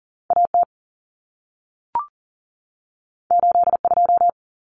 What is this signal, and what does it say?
Morse code, heard as a keyed tone, about 40 wpm, 700 Hz. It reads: AT 82
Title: Common Cut Numbers - 2-Digit 40wpm